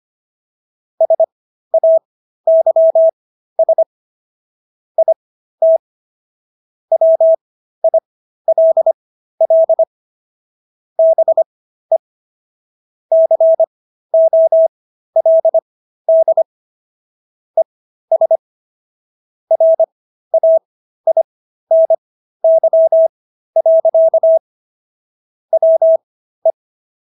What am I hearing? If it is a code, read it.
Morse code, heard as a keyed tone, about 25 wpm, spaced as for 13 wpm, 650 Hz. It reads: SAYS IT WILL BE COLD ES RAINY. WE